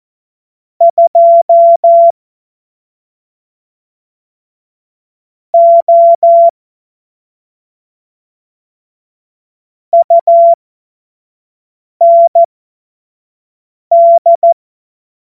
Morse code, heard as a keyed tone, about 14 wpm, spaced as for 4 wpm, 680 Hz. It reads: 2O UND
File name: Morsekurs / Witze / Witz 06. Januar 2026 25 F70